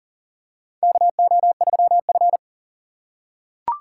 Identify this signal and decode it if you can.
Morse code, heard as a keyed tone, about 40 wpm, 700 Hz. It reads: KO3F
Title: USA Call Signs - 4 Characters - 6x Character Spacing - 40wpm